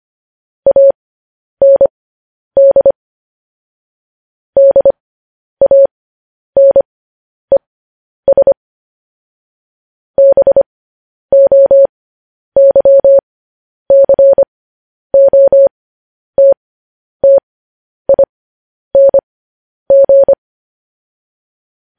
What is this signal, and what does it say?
Morse code, heard as a keyed tone, about 25 wpm, spaced as for 10 wpm, 550 Hz. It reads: AND DANES BOYCOTTING
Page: news headlines in morse code at 10 wpm